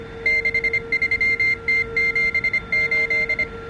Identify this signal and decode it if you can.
Morse code, heard as a keyed tone, about 25 wpm, 2100 Hz, with noise. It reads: 63T78